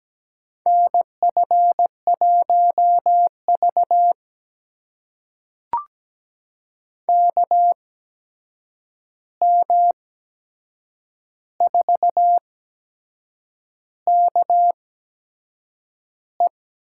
Morse code, heard as a keyed tone, about 17 wpm, spaced as for 7 wpm, 700 Hz. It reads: NF1V KM4KE
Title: USA Call Signs - 8x Character Spacing - 17wpm